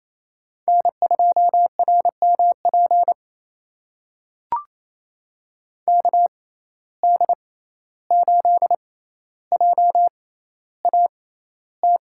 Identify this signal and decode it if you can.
Morse code, heard as a keyed tone, about 28 wpm, spaced as for 14 wpm, 700 Hz. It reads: N2RMP KD8JAT
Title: USA Call Signs - 6x Character Spacing - 28wpm